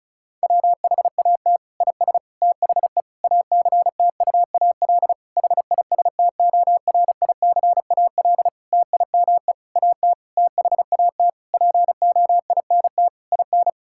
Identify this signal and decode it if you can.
Morse code, heard as a keyed tone, about 34 wpm, 700 Hz. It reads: WHAT IS THE ACTUAL HISTORICAL TIME AT THAT POINT IN
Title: Sentences from Top 1000 Words - Review Part 5 - 35wpm